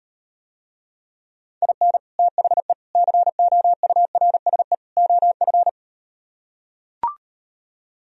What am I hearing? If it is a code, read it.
Morse code, heard as a keyed tone, about 39 wpm, 700 Hz. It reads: IN THE COURSE OF